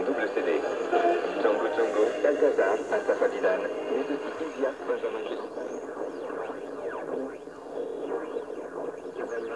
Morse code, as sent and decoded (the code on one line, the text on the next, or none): none